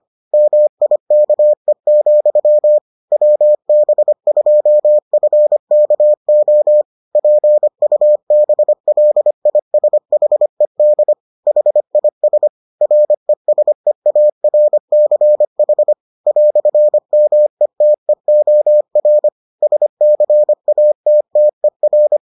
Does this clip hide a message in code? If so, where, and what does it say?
Yes: Morse code from 0.3 to 22.2 s, heard as a keyed tone, about 25 wpm, 600 Hz. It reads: MIKE, WB2FKO PUBLISHED HIS RESEARCH "METEOR SCATTER